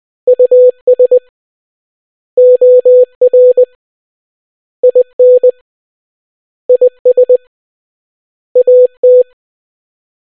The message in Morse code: US OR IN IS AT